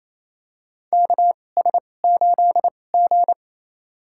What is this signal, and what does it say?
Morse code, heard as a keyed tone, about 27 wpm, spaced as for 20 wpm, 700 Hz. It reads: KS8G
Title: USA Call Signs - 4 Characters - 2x Character Spacing - 28wpm